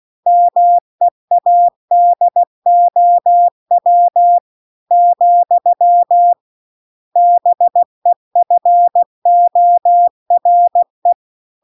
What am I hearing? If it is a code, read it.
Morse code, heard as a keyed tone, about 16 wpm, 700 Hz. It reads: MEADOW , BEFORE